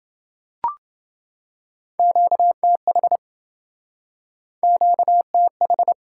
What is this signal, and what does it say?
Morse code, heard as a keyed tone, about 28 wpm, 700 Hz. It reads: QTH QTH